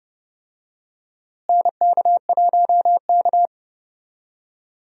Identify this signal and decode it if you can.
Morse code, heard as a keyed tone, about 30 wpm, 700 Hz. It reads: NK1K
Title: USA Call Signs - 4 Characters - Repeated 2x - 30wpm